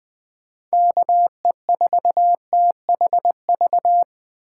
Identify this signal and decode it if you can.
Morse code, heard as a keyed tone, about 20 wpm, 700 Hz. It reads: KE4THV
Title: USA Call Signs - 6 Characters - 4x Character Spacing - 20wpm